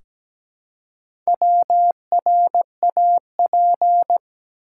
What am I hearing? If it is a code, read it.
Morse code, heard as a keyed tone, about 17 wpm, 700 Hz. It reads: WRAP
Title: Four Letter Words - 17wpm